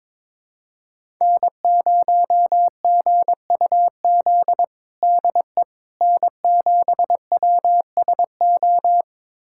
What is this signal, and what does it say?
Morse code, heard as a keyed tone, about 22 wpm, 700 Hz. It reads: N0GUZ DE N7WSO
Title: USA Callsign DE Callsign - 5 Characters - 22wpm